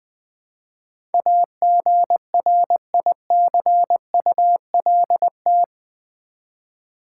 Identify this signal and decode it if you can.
Morse code, heard as a keyed tone, about 20 wpm, 700 Hz. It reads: AGRICULT
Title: Build A Word - 4 Letters At A Time - 11-Letter Words - 20wpm